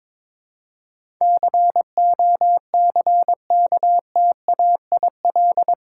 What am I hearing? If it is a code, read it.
Morse code, heard as a keyed tone, about 22 wpm, 700 Hz. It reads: COCKTAIL
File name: practice / 22wpm / Eight Letter Words - 22wpm